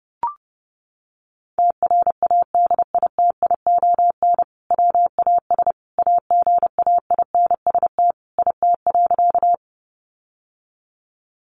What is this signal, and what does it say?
Morse code, heard as a keyed tone, about 30 wpm, 700 Hz. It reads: TRADITION WAS AGAINST IT.
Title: Sentences from Top 1500 Words 30wpm